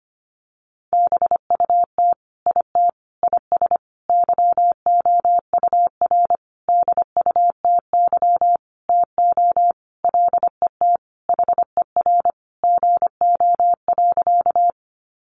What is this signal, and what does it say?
Morse code, heard as a keyed tone, about 25 wpm, 700 Hz. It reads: BUT IT IS YOUR DUTY TO LET HER GO.
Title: Sentences from Top 1500 Words 25wpm